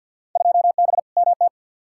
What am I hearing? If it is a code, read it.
Morse code, heard as a keyed tone, about 50 wpm, 700 Hz. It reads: 2B MT
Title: Field Day Exchanges Encore - 50wpm